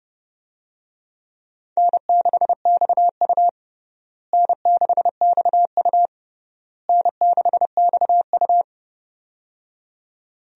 Morse code, heard as a keyed tone, about 30 wpm, 700 Hz. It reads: N6XU N6XU N6XU
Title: USA Call Signs - 4 Characters - Repeated 3x - 30wpm